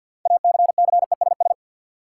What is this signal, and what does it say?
Morse code, heard as a keyed tone, about 53 wpm, 700 Hz. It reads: ACCESS
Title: Six Letter Words - 50wpm